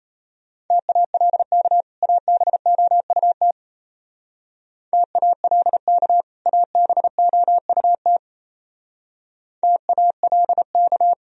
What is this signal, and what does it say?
Morse code, heard as a keyed tone, about 32 wpm, 700 Hz. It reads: TALK ABOUT TALK ABOUT TALK
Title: Speed-Racing - Sets of 2 Words 28wpm